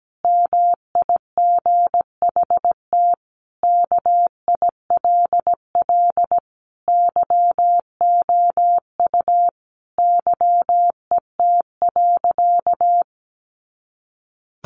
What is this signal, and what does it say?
Morse code, heard as a keyed tone, about 17 wpm, 700 Hz. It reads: MIGHT KILL YOU YET.
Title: A Sound Of Thunder 17wpm